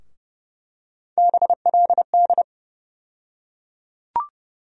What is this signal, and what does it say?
Morse code, heard as a keyed tone, about 30 wpm, 700 Hz. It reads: BLD
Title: USA Call Sign Postfixes - 3 Letters 30wpm